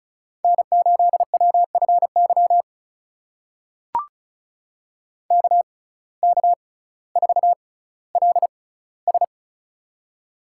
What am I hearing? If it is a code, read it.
Morse code, heard as a keyed tone, about 36 wpm, spaced as for 18 wpm, 700 Hz. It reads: N8WFY KK4LS
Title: USA Call Signs - 5 Characters - 6x Character Spacing - 35wpm